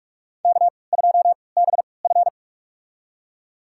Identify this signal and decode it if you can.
Morse code, heard as a keyed tone, about 46 wpm, spaced as for 25 wpm, 700 Hz. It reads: K2BF
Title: USA Call Signs - 3x Character Spacing - 45wpm